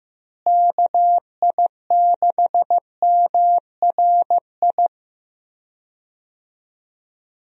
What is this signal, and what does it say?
Morse code, heard as a keyed tone, about 15 wpm, 700 Hz. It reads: KI6MRI
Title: USA Call Signs - 6 Characters - Repeated 3x - 15wpm